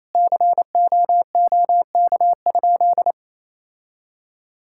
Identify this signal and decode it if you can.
Morse code, heard as a keyed tone, about 27 wpm, 700 Hz. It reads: COOK?
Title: Sentences from Top 1000 Words - Review Part 5 - 28wpm